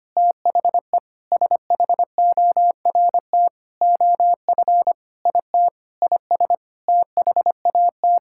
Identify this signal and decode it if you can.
Morse code, heard as a keyed tone, about 25 wpm, 700 Hz. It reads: THE SHORT OF IT IS THAT
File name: practice / 25wpm / Sentences with 9-Word Phrases - 25wpm